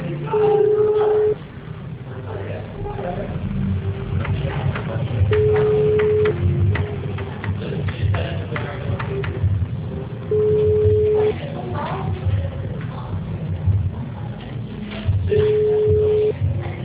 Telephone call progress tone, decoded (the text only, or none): ringback tone